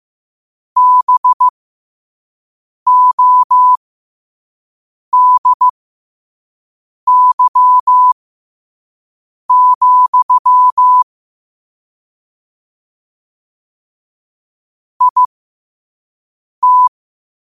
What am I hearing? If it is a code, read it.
Morse code, heard as a keyed tone, about 15 wpm, spaced as for 5 wpm, 1000 Hz. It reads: BODY, IT
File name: qotdfiles / 5wpm